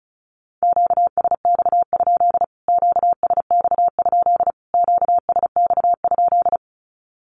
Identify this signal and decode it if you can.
Morse code, heard as a keyed tone, about 34 wpm, 700 Hz. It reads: QSX? QSX? QSX?
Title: Q-Codes Repeated 3x 35wpm